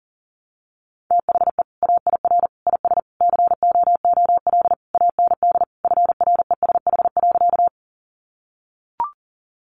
Morse code, heard as a keyed tone, about 40 wpm, 700 Hz. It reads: THE AIR IS COOL AND FRESH.
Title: Sentences from Top 1200 Words 40wpm